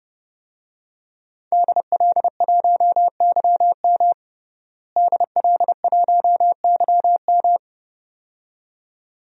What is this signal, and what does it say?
Morse code, heard as a keyed tone, about 30 wpm, 700 Hz. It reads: DL1YM DL1YM
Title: DXCC-100 Call Signs - 5 Characters - Repeated 2x - 30wpm